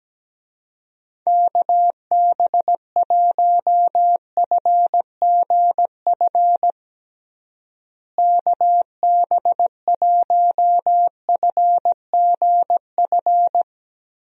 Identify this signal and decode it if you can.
Morse code, heard as a keyed tone, about 17 wpm, 700 Hz. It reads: KB1FGF KB1FGF
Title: USA Call Signs - 6 Characters - Repeated 2x - 17wpm